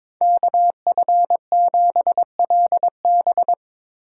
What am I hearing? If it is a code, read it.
Morse code, heard as a keyed tone, about 24 wpm, 700 Hz. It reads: KF7LB
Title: Speed Racing - USA 2x2 Call Signs 22wpm